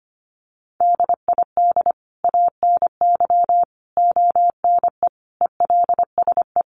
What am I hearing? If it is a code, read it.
Morse code, heard as a keyed tone, about 25 wpm, 700 Hz. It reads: DID ANY ONE ELSE
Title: Sentences from Top 1000 Words 25wpm